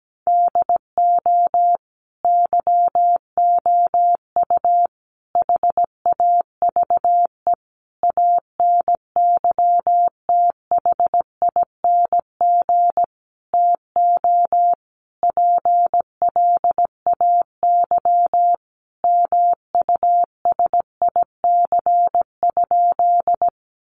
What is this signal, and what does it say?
Morse code, heard as a keyed tone, about 17 wpm, 700 Hz. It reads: DO YOU HAVE ANYTHING TO PLAY MUSIC?
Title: Sentences from Top 1000 Words 17wpm